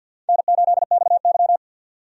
Morse code, heard as a keyed tone, about 52 wpm, 700 Hz. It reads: N8XY
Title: USA Call Signs - 4 Characters - 50wpm